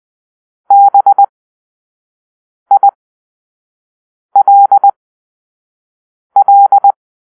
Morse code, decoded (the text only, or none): BILL